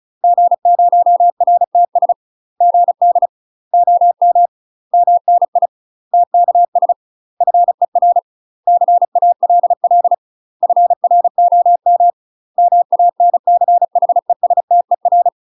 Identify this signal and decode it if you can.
Morse code, heard as a keyed tone, about 34 wpm, 700 Hz. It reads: G0RTS GD OM MNI TKS FER CALL FROM MANCHESTER
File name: UK MASTER QSOs TRAINING / 35WPM / UK MASTER QSOs 0039